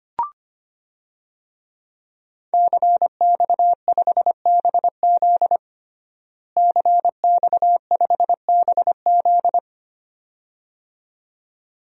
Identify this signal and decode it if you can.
Morse code, heard as a keyed tone, about 25 wpm, 700 Hz. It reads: CX5BZ CX5BZ
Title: DXCC-100 Call Signs - 5 Characters - Repeated 2x - 25wpm